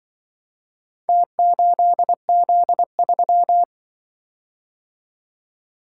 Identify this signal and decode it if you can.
Morse code, heard as a keyed tone, about 24 wpm, 700 Hz. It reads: T8Z3
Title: Speed-Racing - Four Letter-Number Combinations - 20wpm